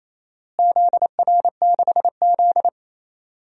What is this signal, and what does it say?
Morse code, heard as a keyed tone, about 28 wpm, 700 Hz. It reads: ZR6Z
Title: DXCC-100 Call Signs - 4 Characters - Repeated 2x - 28wpm